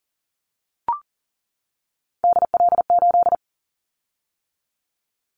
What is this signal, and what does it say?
Morse code, heard as a keyed tone, about 40 wpm, 700 Hz. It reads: DL8
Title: Callsign Prefixes for Top 25 DXCC - 40wpm